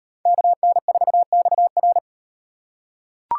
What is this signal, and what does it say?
Morse code, heard as a keyed tone, about 38 wpm, 700 Hz. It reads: KN4XR